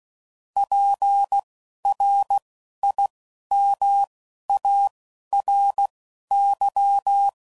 Morse code, heard as a keyed tone, about 16 wpm, spaced as for 11 wpm, 800 Hz. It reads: PRIMARY
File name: CW Code Groups @10WPM / Technical Subjects / SwPwrSup